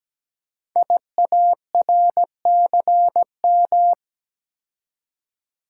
Morse code, heard as a keyed tone, about 17 wpm, 700 Hz. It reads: IARCM
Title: Code Groups - 5 Letters - 17wpm